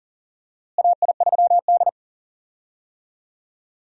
Morse code, heard as a keyed tone, about 40 wpm, 700 Hz. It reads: AI3D
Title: USA Call Signs - 4 Characters - Repeated 3x - 40wpm